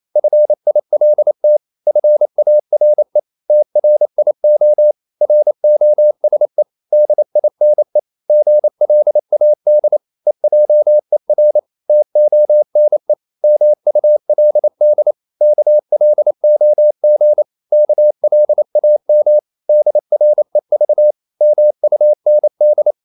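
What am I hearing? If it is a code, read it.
Morse code, heard as a keyed tone, about 28 wpm, 600 Hz. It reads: FILT FARE TRIO ROSE DINE GLAD EJER TONE MULD KLOG KLAM DREV MUND
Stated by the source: Danish short words sent in Morse at 28 wpm, from CW med Gnister.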